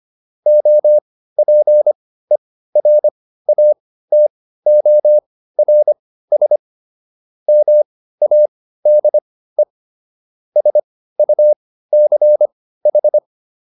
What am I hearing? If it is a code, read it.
Morse code, heard as a keyed tone, about 25 wpm, spaced as for 15 wpm, 600 Hz. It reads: OPERATORS MADE SUCH